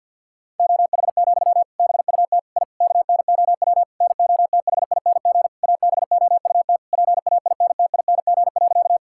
Morse code, heard as a keyed tone, about 50 wpm, 700 Hz. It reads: OH, BUT I KNOW NOTHING ABOUT PAINTING.
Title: Sentences from Top 1000 Words - Review Part 6 - 50wpm